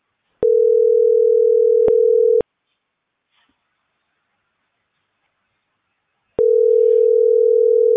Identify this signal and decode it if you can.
Ringback tone (North America), 440 + 480 Hz.